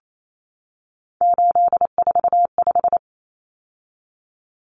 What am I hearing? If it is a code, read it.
Morse code, heard as a keyed tone, about 28 wpm, 700 Hz. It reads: 845
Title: Common Cut Numbers - 3-Digit 28wpm